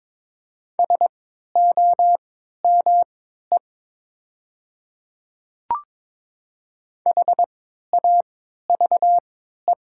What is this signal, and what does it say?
Morse code, heard as a keyed tone, about 22 wpm, spaced as for 12 wpm, 700 Hz. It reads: SOME HAVE
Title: 3x Character Spacing - Top 200 Words - 22wpm